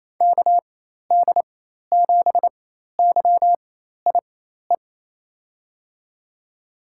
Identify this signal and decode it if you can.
Morse code, heard as a keyed tone, about 28 wpm, spaced as for 13 wpm, 700 Hz. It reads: KD7YIE
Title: USA Call Signs - 6 Characters - 4x Character Spacing - 28wpm